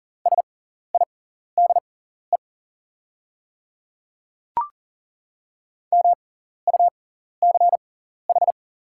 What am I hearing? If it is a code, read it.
Morse code, heard as a keyed tone, about 40 wpm, spaced as for 14 wpm, 690 Hz. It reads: SIDE MUCH
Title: 6x Character Spacing - Top 300 Words - 40wpm